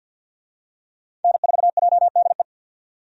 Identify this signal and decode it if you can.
Morse code, heard as a keyed tone, about 47 wpm, 700 Hz. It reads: N4JDE
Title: USA Call Signs - 5 Characters - 50wpm